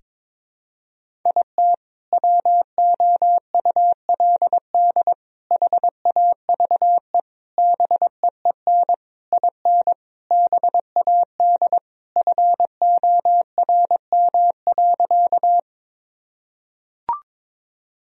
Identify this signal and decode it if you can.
Morse code, heard as a keyed tone, about 22 wpm, 700 Hz. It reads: IT WOULD HAVE BEEN IN BAD FORM.
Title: Sentences from Top 1000 Words - Review Part 1 - 22wpm